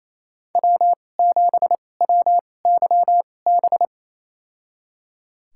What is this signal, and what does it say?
Morse code, heard as a keyed tone, about 28 wpm, spaced as for 20 wpm, 700 Hz. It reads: W7WYB
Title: USA Call Signs - 2x Character Spacing - 28wpm